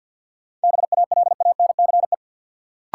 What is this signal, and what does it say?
Morse code, heard as a keyed tone, about 53 wpm, 700 Hz. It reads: BALANCE
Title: Sentences from Top 900 Words - New Word Drill - 50wpm